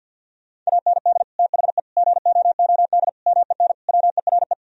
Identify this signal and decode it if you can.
Morse code, heard as a keyed tone, about 50 wpm, 700 Hz. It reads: AND THE GOOD MEN WERE
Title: Sentences from Top 1000 Words - Review Part 2 - 50wpm